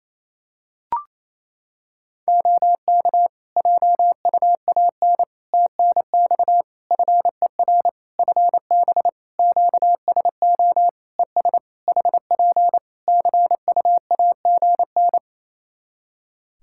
Morse code, heard as a keyed tone, about 28 wpm, 700 Hz. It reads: OK JUAN TNX FER FB QSO ES HP CUAGN
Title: Practice QSO - Exchange 4 - Ending - 28wpm